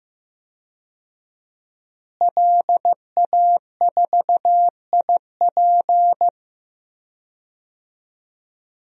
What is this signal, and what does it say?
Morse code, heard as a keyed tone, about 15 wpm, 700 Hz. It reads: LA4IP
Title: DXCC-100 Call Signs - 5 Characters - Repeated 2x - 15wpm